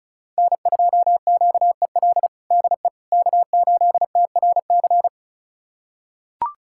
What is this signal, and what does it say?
Morse code, heard as a keyed tone, about 34 wpm, 700 Hz. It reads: N2QEL DE K8TRC
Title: USA Callsign DE Callsign - 5 Characters - 35wpm